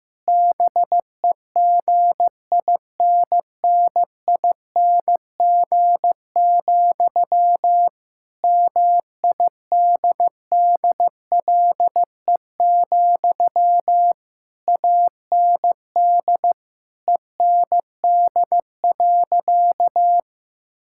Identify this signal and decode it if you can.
Morse code, heard as a keyed tone, about 16 wpm, 700 Hz. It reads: BEGINNING, MIDDLE, AND END.
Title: Sentences from Top 1000 Words - Review Part 6 - 15wpm